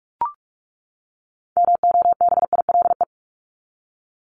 Morse code, heard as a keyed tone, about 44 wpm, 700 Hz. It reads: MOBILE
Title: Top 1751-2000 Words 45wpm